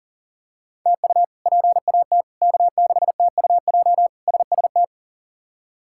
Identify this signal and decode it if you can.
Morse code, heard as a keyed tone, about 40 wpm, 700 Hz. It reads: TU PAT K6TUJ SST
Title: K1USN SST QSO Practice - 40wpm